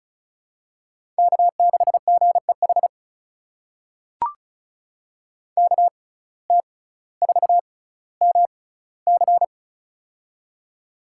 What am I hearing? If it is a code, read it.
Morse code, heard as a keyed tone, about 36 wpm, spaced as for 18 wpm, 700 Hz. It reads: K6GEH KT4MC
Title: USA Call Signs - 5 Characters - 6x Character Spacing - 35wpm